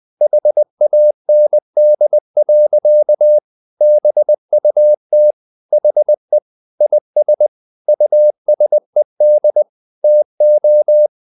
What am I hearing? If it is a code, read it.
Morse code, heard as a keyed tone, about 20 wpm, 600 Hz. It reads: HAND. BUT HE IS USED TO
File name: the old man and the sea part 2 20wpm